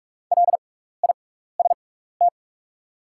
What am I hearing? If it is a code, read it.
Morse code, heard as a keyed tone, about 45 wpm, spaced as for 15 wpm, 700 Hz. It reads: LIST
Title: 6x Character Spacing - Top 500 Words - 45wpm